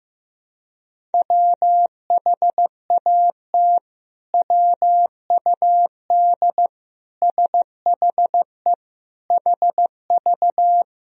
Morse code, heard as a keyed tone, about 15 wpm, 700 Hz. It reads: WHAT WUD SHE HV
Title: Sentences from Top 100 Words and Abbreviations 15wpm